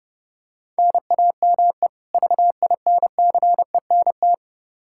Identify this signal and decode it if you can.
Morse code, heard as a keyed tone, about 30 wpm, 700 Hz. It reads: NAME VINCENT
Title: Practice QSO - Exchange 1 - Beginning - 30wpm